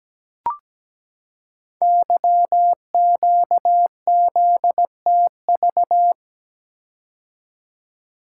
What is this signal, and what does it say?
Morse code, heard as a keyed tone, about 17 wpm, 700 Hz. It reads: YQZTV
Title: Code Groups - 5 Letter:Numbers - 17wpm